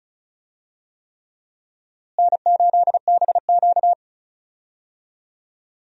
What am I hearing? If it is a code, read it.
Morse code, heard as a keyed tone, about 34 wpm, 700 Hz. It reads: N8BQ
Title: USA Call Signs - 4 Characters - Repeated 3x - 35wpm